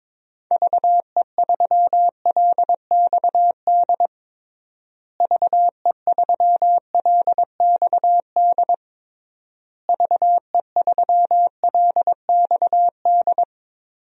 Morse code, heard as a keyed tone, about 21 wpm, 700 Hz. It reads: VE3LXD VE3LXD VE3LXD
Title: DXCC-100 Call Signs - 6 Characters - Repeated 3x - 22wpm